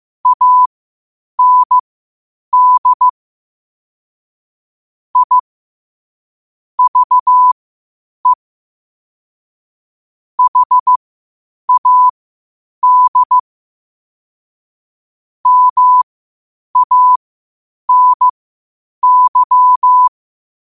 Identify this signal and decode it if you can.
Morse code, heard as a keyed tone, about 15 wpm, spaced as for 8 wpm, 1000 Hz. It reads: AND I VE HAD MANY
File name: qotdfiles / 8wpm